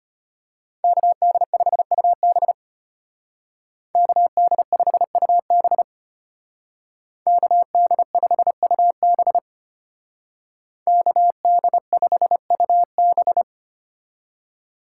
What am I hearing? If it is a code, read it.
Morse code, heard as a keyed tone, about 33 wpm, 700 Hz. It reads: KD5UB KD5UB KD5UB KD5UB